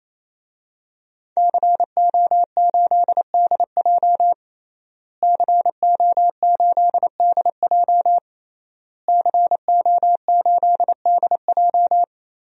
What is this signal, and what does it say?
Morse code, heard as a keyed tone, about 28 wpm, 700 Hz. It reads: CO8DJ CO8DJ CO8DJ
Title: DXCC-100 Call Signs - 5 Characters - Repeated 3x - 28wpm